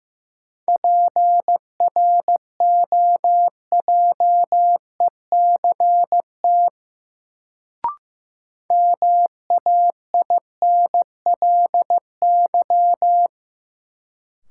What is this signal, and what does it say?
Morse code, heard as a keyed tone, about 15 wpm, 700 Hz. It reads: PROJECT MAINLY
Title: Sentences from Top 1400 Words - New Word Drill Alt - 15wpm